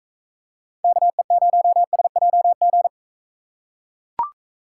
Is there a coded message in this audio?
Yes: Morse code from 0.8 to 2.9 s, heard as a keyed tone, about 39 wpm, 700 Hz. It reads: KE0SJG